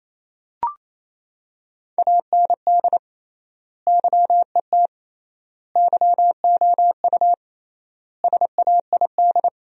Morse code, heard as a keyed tone, about 28 wpm, 700 Hz. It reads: AND YET YOU SAID
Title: 3x Word-Spacing - Sentences from Top 500 Words - 28wpm